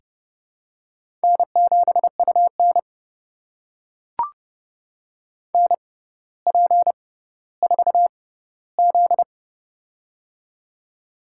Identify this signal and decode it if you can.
Morse code, heard as a keyed tone, about 29 wpm, spaced as for 15 wpm, 700 Hz. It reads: N7UN NP4Z
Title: USA Call Signs - 4 Characters - 6x Character Spacing - 30wpm